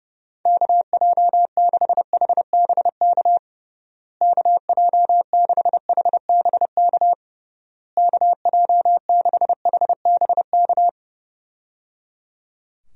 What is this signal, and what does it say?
Morse code, heard as a keyed tone, about 30 wpm, 700 Hz. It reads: KJ6HBK KJ6HBK KJ6HBK
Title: USA Call Signs - 6 Characters - Repeated 3x - 30wpm